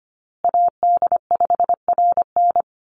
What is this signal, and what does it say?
Morse code, heard as a keyed tone, about 25 wpm, 700 Hz. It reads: AD5RN
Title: USA Call Signs Repeated 2x 25wpm